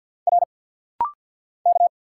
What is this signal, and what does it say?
Morse code, heard as a keyed tone, about 50 wpm, 700 Hz. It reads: R K
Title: Single Letter-Number Alt - Rapid-Fire 50wpm